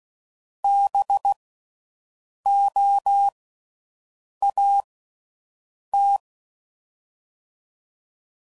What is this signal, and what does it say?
Morse code, heard as a keyed tone, about 16 wpm, 800 Hz. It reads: B O A T
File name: Suffixes / General / Houses